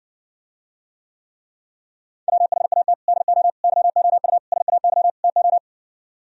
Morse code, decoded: WHAT DO YOU SAY TO

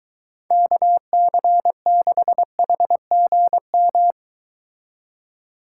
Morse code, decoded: KC6HGM

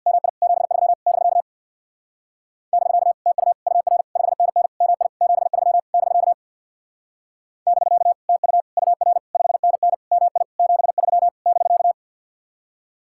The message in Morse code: MI 73 <BK> <BK> TU UR 5NN MI 73 <BK> <BK> TU UR 5NN MI 73 <BK>

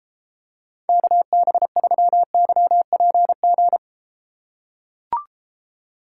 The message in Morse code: KB3YPG

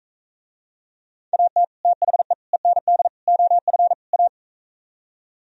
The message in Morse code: AT THE END OF A

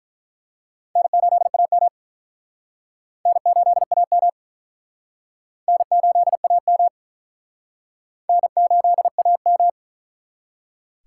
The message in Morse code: N8AM N8AM N8AM N8AM

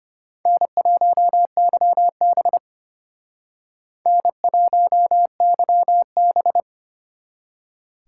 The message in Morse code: N1YB N1YB

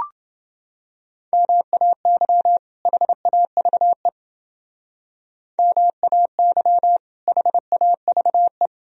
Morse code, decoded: MAY HAVE MAY HAVE

